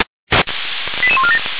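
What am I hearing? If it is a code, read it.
Five-tone selective call (ZVEI), heard as quick runs of five tones, tones of about 70 ms. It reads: 88147
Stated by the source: ZVEI five-tone call received on SDR for Feuerwehr Orlamünde.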